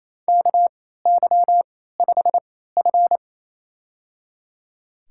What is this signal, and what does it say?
Morse code, heard as a keyed tone, about 28 wpm, spaced as for 16 wpm, 700 Hz. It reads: KY5F
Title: USA Call Signs - 4 Characters - 3x Character Spacing - 28wpm